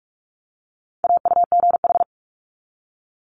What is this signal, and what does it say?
Morse code, heard as a keyed tone, about 46 wpm, 700 Hz. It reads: AUGH
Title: Multi-letter Phonemes 45wpm